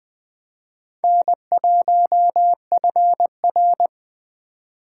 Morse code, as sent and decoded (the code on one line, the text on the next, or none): -. .---- ..-. .-.
N1FR